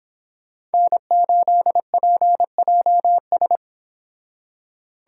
Morse code, heard as a keyed tone, about 30 wpm, 700 Hz. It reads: N8PJS